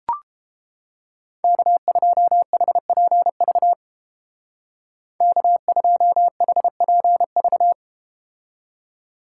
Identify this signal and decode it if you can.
Morse code, heard as a keyed tone, about 32 wpm, 700 Hz. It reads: K2HPV K2HPV